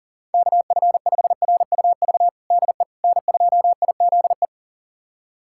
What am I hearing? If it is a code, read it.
Morse code, heard as a keyed tone, about 37 wpm, 700 Hz. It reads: KF5RUV DE N2IZE
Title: USA Callsign DE Callsign 40wpm